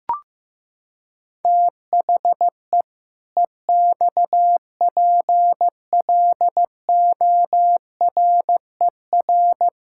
Morse code, heard as a keyed tone, about 15 wpm, 700 Hz. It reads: THE EXPLORER
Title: Pangram Challenge v2 - 15wpm